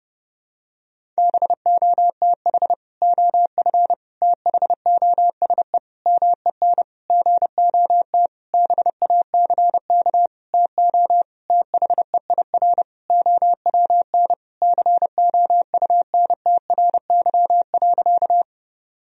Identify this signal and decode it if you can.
Morse code, heard as a keyed tone, about 28 wpm, 700 Hz. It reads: BOTH OF THOSE MEN GOT BACK TO THEIR OWN COUNTRY.
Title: Sentences from Top 1000 Words - Review Part 1 - 30wpm